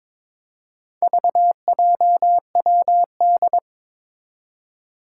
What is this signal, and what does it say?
Morse code, heard as a keyed tone, about 22 wpm, 700 Hz. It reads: VJWD
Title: Four Challenging Letter-Number Combinations 22wpm